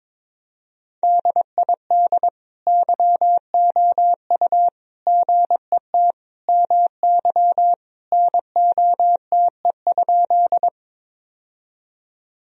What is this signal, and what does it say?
Morse code, heard as a keyed tone, about 22 wpm, 700 Hz. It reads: DID YOU GET MY NOTE?
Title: Sentences from Top 1000 Words - Review Part 2 - 22wpm